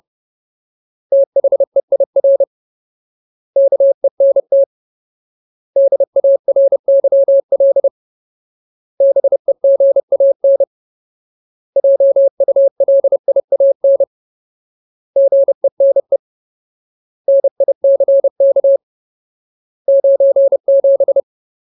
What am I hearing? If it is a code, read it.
Morse code, heard as a keyed tone, about 30 wpm, 550 Hz. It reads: THEIR KENT DARYL BEGAN JULIAN GENE NICK 97
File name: Ultimate Word List / 30WPM Ultimate